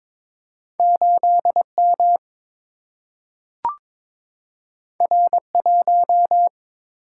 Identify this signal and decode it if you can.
Morse code, heard as a keyed tone, about 22 wpm, 700 Hz. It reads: 8 M R 1
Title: Single-Letter Single-Number 22wpm